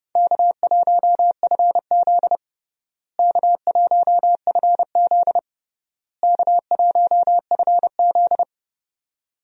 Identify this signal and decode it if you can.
Morse code, heard as a keyed tone, about 32 wpm, 700 Hz. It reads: K1FZ K1FZ K1FZ
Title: USA Call Signs - 4 Characters - Repeated 3x - 30wpm